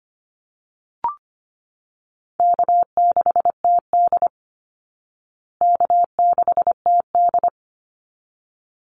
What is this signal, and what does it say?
Morse code, heard as a keyed tone, about 25 wpm, 700 Hz. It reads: K6TD K6TD
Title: Easy Call Signs Repeated 2x - 25wpm